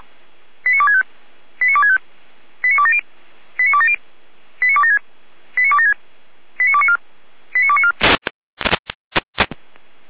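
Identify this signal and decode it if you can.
Five-tone selective call (ZVEI), heard as quick runs of five tones, tones of about 70 ms. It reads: 89265 89265 89270 89270 89276 89276 89294 89294
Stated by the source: ZVEI five-tone selective call 89250 received by SDR, logged on the Feuerwehr Orlamünde fire brigade server.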